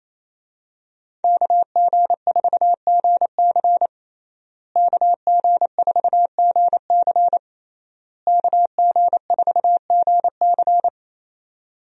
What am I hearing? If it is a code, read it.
Morse code, heard as a keyed tone, about 27 wpm, 700 Hz. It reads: KG4GC KG4GC KG4GC
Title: USA Call Signs - 5 Characters - Repeated 3x - 28wpm